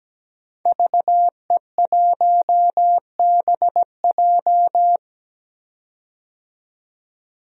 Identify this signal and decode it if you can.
Morse code, heard as a keyed tone, about 17 wpm, 700 Hz. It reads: VE1BJ
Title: DXCC-100 Call Signs - 5 Characters - 17wpm